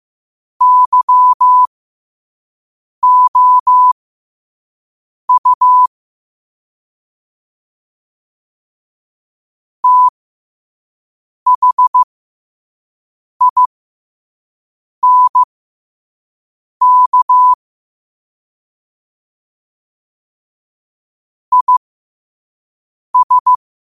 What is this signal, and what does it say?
Morse code, heard as a keyed tone, about 15 wpm, spaced as for 5 wpm, 1000 Hz. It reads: YOU THINK IS
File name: qotdfiles / 5wpm